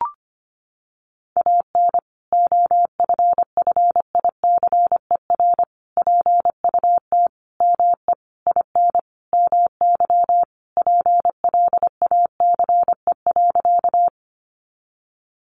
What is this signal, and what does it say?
Morse code, heard as a keyed tone, about 25 wpm, 700 Hz. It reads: AN OFFICER PUT ME IN MY PLACE.
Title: Sentences from Top 1200 Words 25wpm